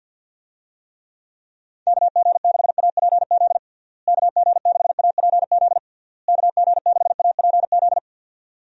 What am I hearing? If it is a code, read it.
Morse code, heard as a keyed tone, about 52 wpm, 700 Hz. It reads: KG6APZ KG6APZ KG6APZ
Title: USA Call Signs - 6 Characters - Repeated 3x - 50wpm